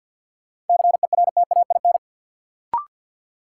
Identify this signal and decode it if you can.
Morse code, heard as a keyed tone, about 53 wpm, 700 Hz. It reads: CERTAIN
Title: Sentences from Top 500 Words - New Word Drill - 50wpm